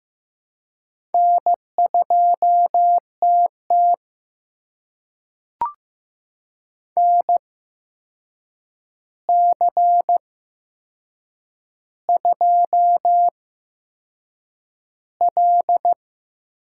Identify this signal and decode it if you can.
Morse code, heard as a keyed tone, about 15 wpm, spaced as for 6 wpm, 700 Hz. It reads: N2TT NC2L
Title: USA Call Signs - 4 Characters - 8x Character Spacing - 15wpm